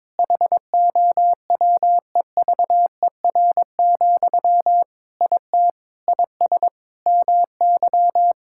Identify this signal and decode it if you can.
Morse code, heard as a keyed tone, about 22 wpm, 700 Hz. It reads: HOWEVER, IT IS MY
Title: Sentences from Top 1600 Words 22wpm